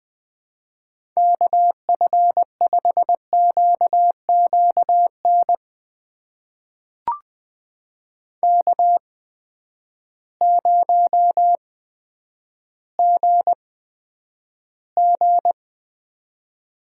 Morse code, heard as a keyed tone, about 20 wpm, spaced as for 10 wpm, 700 Hz. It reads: KF5QQN K0GG
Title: USA Call Signs - 8x Character Spacing - 20wpm